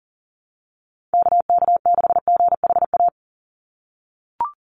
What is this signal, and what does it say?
Morse code, heard as a keyed tone, about 40 wpm, 700 Hz. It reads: KK6GHA
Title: Call Signs - 40wpm